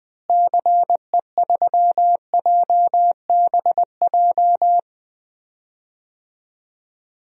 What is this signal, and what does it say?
Morse code, heard as a keyed tone, about 20 wpm, 700 Hz. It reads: CE3JBJ
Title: DXCC-100 Call Signs - 6 Characters - Repeated 2x - 20wpm